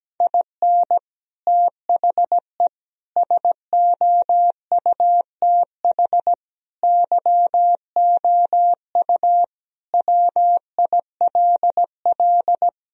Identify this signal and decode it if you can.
Morse code, heard as a keyed tone, about 17 wpm, 700 Hz. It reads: IN THE SOUTH YOU WILL
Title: Sentences from Top 1000 Words - Review Part 2 - 17wpm